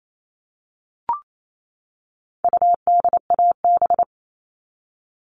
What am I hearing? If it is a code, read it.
Morse code, heard as a keyed tone, about 28 wpm, 700 Hz. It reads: UDAB
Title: All Cut Numbers - 4-Digit 28wpm